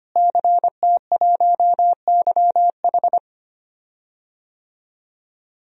Morse code, heard as a keyed tone, about 25 wpm, 700 Hz. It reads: CT1YH
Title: DXCC-100 Call Signs - 5 Characters - Repeated 2x - 25wpm